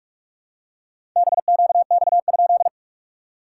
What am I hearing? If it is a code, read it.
Morse code, heard as a keyed tone, about 46 wpm, 690 Hz. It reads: BQX?
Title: SOTA QSO Practice 45wpm